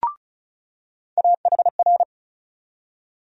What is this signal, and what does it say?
Morse code, heard as a keyed tone, about 36 wpm, 700 Hz. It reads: AHR
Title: USA Call Sign Postfixes - 3 Letters 35wpm